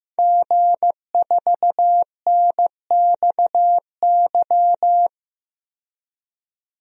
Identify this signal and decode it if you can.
Morse code, heard as a keyed tone, about 15 wpm, 700 Hz. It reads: G4NXY
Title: DXCC-100 Call Signs - 5 Characters - 15wpm